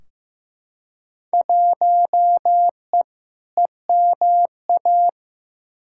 Morse code, heard as a keyed tone, about 15 wpm, 700 Hz. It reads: 1E EMA
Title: Field Day Exchanges Encore - 15wpm